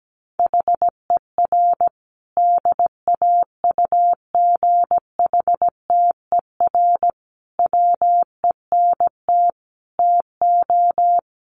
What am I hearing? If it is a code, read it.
Morse code, heard as a keyed tone, about 17 wpm, 700 Hz. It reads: HER DAUGHTER WENT TO
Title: Sentences from Top 1500 Words 17wpm